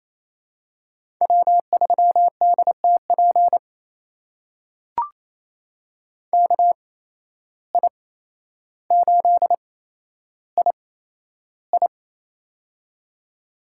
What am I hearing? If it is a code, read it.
Morse code, heard as a keyed tone, about 28 wpm, spaced as for 12 wpm, 700 Hz. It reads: W3DTP KI8II
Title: USA Call Signs - 5 Characters - 8x Character Spacing - 28wpm